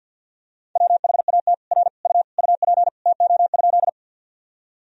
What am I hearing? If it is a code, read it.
Morse code, heard as a keyed tone, about 50 wpm, 700 Hz. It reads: WHAT R U UP TO?
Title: Sentences from Top 100 Words and Abbreviations 50wpm